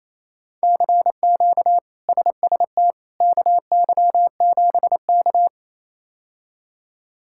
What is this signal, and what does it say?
Morse code, heard as a keyed tone, about 28 wpm, 700 Hz. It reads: CQ SST KY7K
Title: K1USN SST QSO Practice - 28wpm